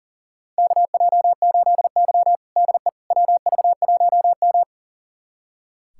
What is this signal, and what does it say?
Morse code, heard as a keyed tone, about 40 wpm, 700 Hz. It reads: KJ8Y DE WV1M
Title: USA Callsign DE Callsign - 4 Characters - 40wpm